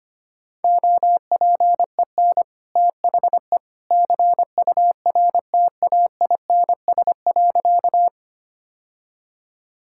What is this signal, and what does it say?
Morse code, heard as a keyed tone, about 25 wpm, 700 Hz. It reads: OPEN THE CURTAINS.
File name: Easy Sentences with Hints - 25wpm